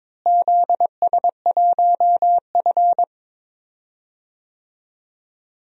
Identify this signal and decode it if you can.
Morse code, heard as a keyed tone, about 22 wpm, 700 Hz. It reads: ZS1F
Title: DXCC-100 Call Signs - 4 Characters - Repeated 3x - 22wpm